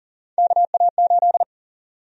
Morse code, heard as a keyed tone, about 40 wpm, 700 Hz. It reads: KA8
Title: USA Call Sign Prefixes - Repeated 2x - 40wpm